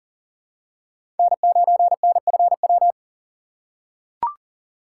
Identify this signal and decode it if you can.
Morse code, heard as a keyed tone, about 40 wpm, 700 Hz. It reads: N9NFW
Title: USA Call Signs - 5 Characters - 2x Character Spacing - 40wpm